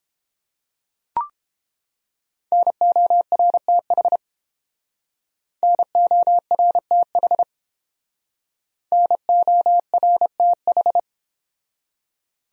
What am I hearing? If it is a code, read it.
Morse code, heard as a keyed tone, about 29 wpm, 700 Hz. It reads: NORTH NORTH NORTH